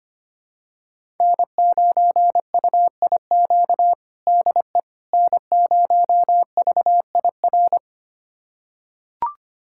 Morse code, heard as a keyed tone, about 25 wpm, 700 Hz. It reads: N9UIQ DE N0VIR
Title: USA Callsign DE Callsign - 5 Characters - 25wpm